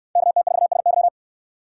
Morse code, heard as a keyed tone, about 67 wpm, 700 Hz. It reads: KT4SY